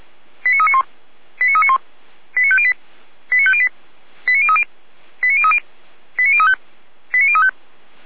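Five-tone selective call (ZVEI), heard as quick runs of five tones, tones of about 70 ms. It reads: 89391 89391 89508 89508 89030 89030 89035 89035
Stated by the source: ZVEI five-tone selective call received on SDR.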